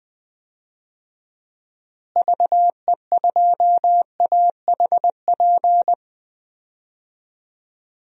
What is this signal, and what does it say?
Morse code, heard as a keyed tone, about 20 wpm, 700 Hz. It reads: VE2AHP
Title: DXCC-100 Call Signs - 6 Characters - Repeated 2x - 20wpm